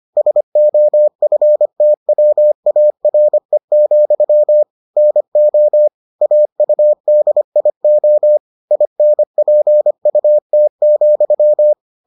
Morse code, heard as a keyed tone, about 25 wpm, 600 Hz. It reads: SOFTWARE, NO AUDIO INPUT,